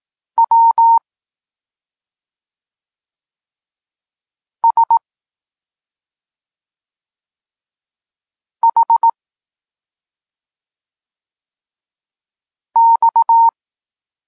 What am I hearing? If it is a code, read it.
Morse code, heard as a keyed tone, about 18 wpm, 920 Hz. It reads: W S H X